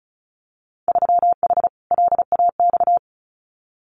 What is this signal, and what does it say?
Morse code, heard as a keyed tone, about 36 wpm, 690 Hz. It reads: 3H LAX
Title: Winter Field Day Exchanges 35wpm